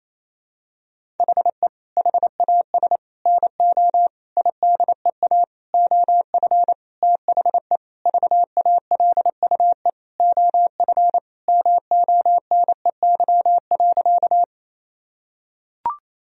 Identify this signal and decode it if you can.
Morse code, heard as a keyed tone, about 28 wpm, 700 Hz. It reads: HE HAS NO IDEA OF THE VALUE OF MONEY.
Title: Sentences from Top 1000 Words - Review Part 3 - 28wpm